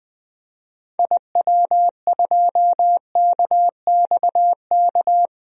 Morse code, heard as a keyed tone, about 20 wpm, 700 Hz. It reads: IW2KXK
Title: DXCC-100 Call Signs - 6 Characters - 20wpm